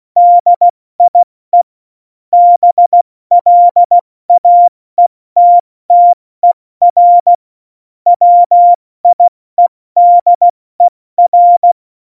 Morse code, heard as a keyed tone, about 16 wpm, 700 Hz. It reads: DIE BLAETTER WIEDER
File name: Morsekurs / Witze / Witz 29. Mai 2018, 70 F80